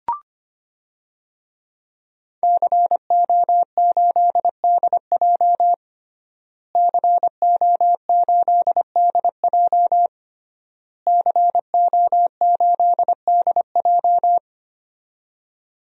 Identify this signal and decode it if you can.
Morse code, heard as a keyed tone, about 25 wpm, 700 Hz. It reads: CO8DJ CO8DJ CO8DJ
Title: DXCC-100 Call Signs - 5 Characters - Repeated 3x - 25wpm